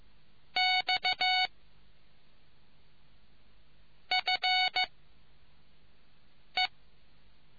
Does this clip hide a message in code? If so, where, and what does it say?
Yes: Morse code from 0.6 to 6.7 s, heard as a keyed tone, about 15 wpm, 2200 Hz. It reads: X F E